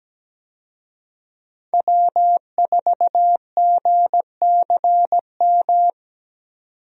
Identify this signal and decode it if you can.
Morse code, heard as a keyed tone, about 17 wpm, 700 Hz. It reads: W4GCM
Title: USA Call Signs - 5 Characters - 17wpm